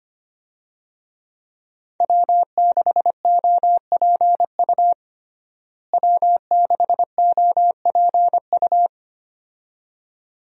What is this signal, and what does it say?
Morse code, heard as a keyed tone, about 25 wpm, 700 Hz. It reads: W6OPU W6OPU
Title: USA Call Signs - 5 Characters - Repeated 2x - 25wpm